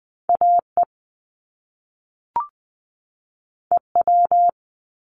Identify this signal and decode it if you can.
Morse code, heard as a keyed tone, about 20 wpm, 700 Hz. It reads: AE EW
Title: Top Two Letter Combinations in Words 20wpm